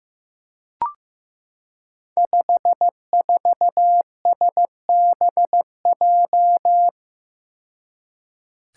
Morse code, heard as a keyed tone, about 15 wpm, 700 Hz. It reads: 54SBJ
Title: Five Challenging Letter-Number Combinations 15wpm